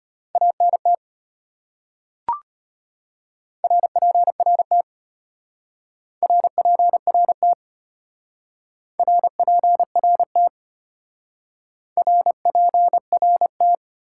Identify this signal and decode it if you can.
Morse code, heard as a keyed tone, about 30 wpm, 700 Hz. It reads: ANT RPRT RPRT RPRT RPRT
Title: Speed-Racing Top 90 QSO Elements 25wpm